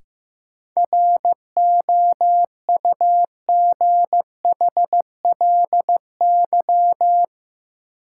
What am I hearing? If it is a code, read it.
Morse code, heard as a keyed tone, about 15 wpm, 700 Hz. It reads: ROUGHLY
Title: Sentences from Top 1500 Words - New Word Drill Alt - 15wpm